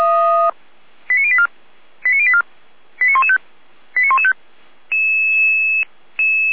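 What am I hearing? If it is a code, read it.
Five-tone selective call (ZVEI), heard as quick runs of five tones, tones of about 70 ms. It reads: 89084 89084 89115 89115
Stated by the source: ZVEI five-tone call 89104 received by SDR, recorded by feuerwehr-orlamuende (f85360) fire brigade alerting server.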